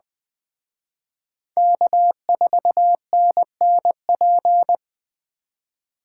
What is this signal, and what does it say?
Morse code, heard as a keyed tone, about 20 wpm, 700 Hz. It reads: K4NNP